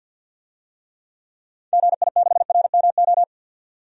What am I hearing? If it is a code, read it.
Morse code, heard as a keyed tone, about 52 wpm, 700 Hz. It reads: GI6RMO
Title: DXCC-100 Call Signs - 6 Characters - 50wpm